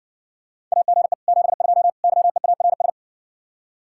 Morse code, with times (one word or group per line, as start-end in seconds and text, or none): 0.7-1.1 AGE
1.3-1.9 72
2.0-2.9 YEARS